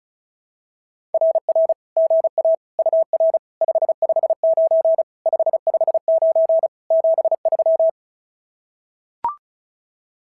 Morse code, heard as a keyed tone, about 34 wpm, 650 Hz. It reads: RR GA UR 559 559 73
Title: SOTA QSO Practice 35wpm